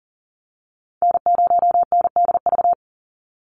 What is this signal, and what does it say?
Morse code, heard as a keyed tone, about 40 wpm, 700 Hz. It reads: N0NDV
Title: USA Call Signs 40wpm